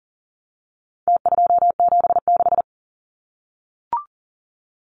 Morse code, heard as a keyed tone, about 40 wpm, 700 Hz. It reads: T276
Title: Numbers - Contest Style - 4-Digit 40wpm